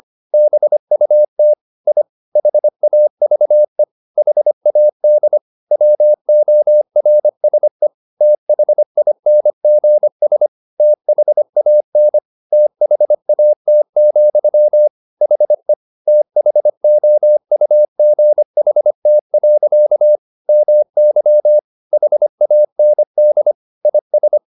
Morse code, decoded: BUT I HAVE HAD WORSE THINGS THAN THAT, HE THOUGHT. MY HAND IS

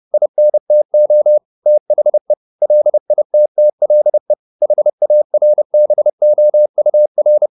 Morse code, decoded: INTO THE LITTLE HARBOUR

